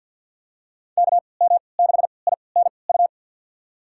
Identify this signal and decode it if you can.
Morse code, heard as a keyed tone, about 49 wpm, spaced as for 28 wpm, 700 Hz. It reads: KM6INU
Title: USA Call Signs - 6 Characters - 3x Character Spacing - 50wpm